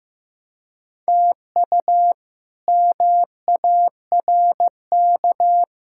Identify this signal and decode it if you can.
Morse code, heard as a keyed tone, about 16 wpm, 700 Hz. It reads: TU MARK
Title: K1USN SST QSO Practice - 15wpm